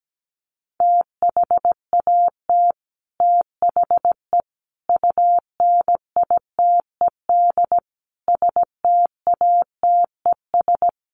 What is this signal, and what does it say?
Morse code, heard as a keyed tone, about 17 wpm, 700 Hz. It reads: THAT THE UNITED STATES
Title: Sets of 5 Words Encore 17wpm